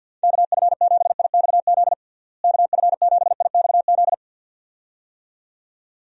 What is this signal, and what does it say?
Morse code, heard as a keyed tone, about 50 wpm, 700 Hz. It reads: KF7IXZ KF7IXZ
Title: USA Call Signs - 6 Characters - Repeated 2x - 50wpm